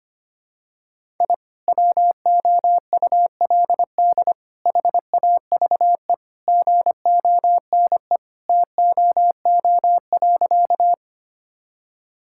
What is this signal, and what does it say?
Morse code, heard as a keyed tone, about 25 wpm, 700 Hz. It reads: I WOULD HAVE GONE TOO.
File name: practice / 25wpm / Sentences with 2-Word Phrases - 25wpm